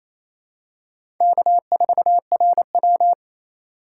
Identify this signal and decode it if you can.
Morse code, heard as a keyed tone, about 28 wpm, 700 Hz. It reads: K4RW
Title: USA Call Signs - 4 Characters - 8x Character Spacing - 28wpm